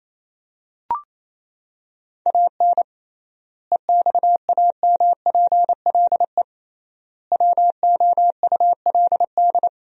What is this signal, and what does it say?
Morse code, heard as a keyed tone, about 28 wpm, 700 Hz. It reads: AN EXAMPLE WOULD
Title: 3x Word-Spacing - Sentences from Top 400 Words - 28wpm